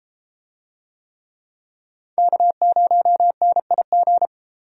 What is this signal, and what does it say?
Morse code, heard as a keyed tone, about 32 wpm, 700 Hz. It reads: K0NIG